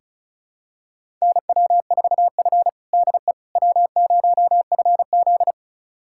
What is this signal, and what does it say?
Morse code, heard as a keyed tone, about 34 wpm, 700 Hz. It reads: NW4F DE W0FZ
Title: USA Callsign DE Callsign - 4 Characters - 35wpm